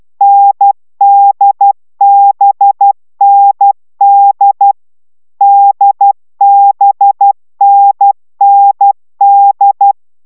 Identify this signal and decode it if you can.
Morse code, heard as a keyed tone, about 12 wpm, 800 Hz. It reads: NDBND DBNND